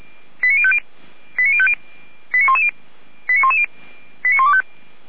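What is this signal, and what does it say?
Five-tone selective call (ZVEI), heard as quick runs of five tones, tones of about 80 ms. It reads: 89050 89110 89125
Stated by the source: Five-tone ZVEI selective call 89097 received on SDR.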